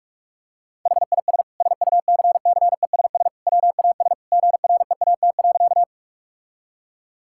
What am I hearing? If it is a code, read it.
Morse code, heard as a keyed tone, about 46 wpm, 700 Hz. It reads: HIS SUCCESS WAS GREAT.
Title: Sentences from Top 1000 Words - Review Part 6 - 45wpm